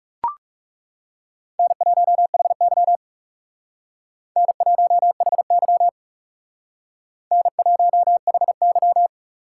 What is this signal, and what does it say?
Morse code, heard as a keyed tone, about 38 wpm, 700 Hz. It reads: N1HY N1HY N1HY
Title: Speed Racing - USA 1x2 Call Signs 30wpm